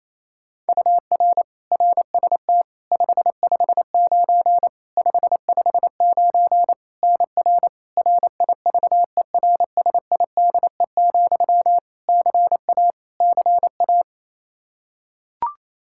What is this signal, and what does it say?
Morse code, heard as a keyed tone, about 28 wpm, 700 Hz. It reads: UR RST 559 559 NR RIVERSIDE, CA CA
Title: Practice QSO - Exchange 1 - Beginning - 28wpm